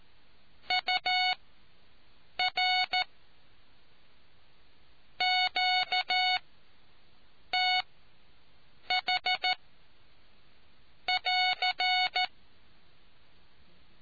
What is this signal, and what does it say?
Morse code, heard as a keyed tone, about 14 wpm, spaced as for 6 wpm, 740 Hz. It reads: UR QTH<AR>